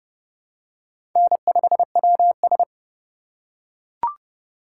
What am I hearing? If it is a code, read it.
Morse code, heard as a keyed tone, about 30 wpm, 700 Hz. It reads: N5WS
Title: USA Call Signs - 4 Characters - 2x Character Spacing - 30wpm